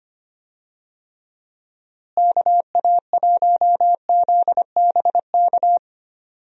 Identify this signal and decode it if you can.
Morse code, heard as a keyed tone, about 25 wpm, 700 Hz. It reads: KA1ZBK
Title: USA Call Signs - 6 Characters - Repeated 3x - 25wpm